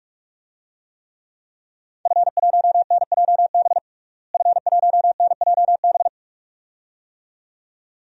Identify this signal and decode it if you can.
Morse code, heard as a keyed tone, about 41 wpm, 700 Hz. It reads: F1NJB F1NJB
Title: DXCC-100 Call Signs - 5 Characters - Repeated 2x - 45wpm